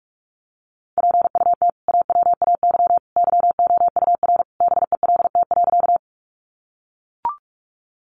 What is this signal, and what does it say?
Morse code, heard as a keyed tone, about 44 wpm, 700 Hz. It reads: PUT AWAY YOUR BELT.
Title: Sentences from Top 1400 Words 45wpm